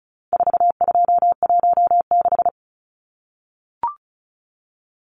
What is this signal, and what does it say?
Morse code, heard as a keyed tone, about 34 wpm, 700 Hz. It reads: 4216
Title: Numbers - 4-Digit 35wpm